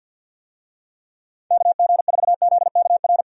Morse code, heard as a keyed tone, about 53 wpm, 700 Hz. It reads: KG4ZKR
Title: USA Call Signs - 6 Characters - 50wpm